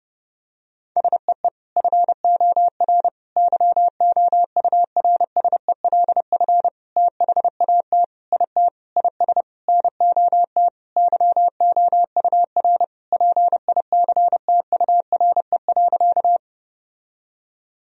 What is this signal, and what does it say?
Morse code, heard as a keyed tone, about 30 wpm, 700 Hz. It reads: SEE FOR YOURSELF THAT IT IS NOT YOUR PICTURE.
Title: Sentences from Top 1000 Words - Review Part 1 - 30wpm